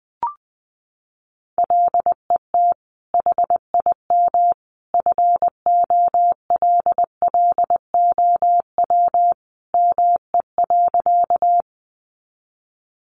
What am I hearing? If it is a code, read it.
Morse code, heard as a keyed tone, about 20 wpm, 700 Hz. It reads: LET HIM FOLLOW ME.
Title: Sentences from Top 300 Words 20wpm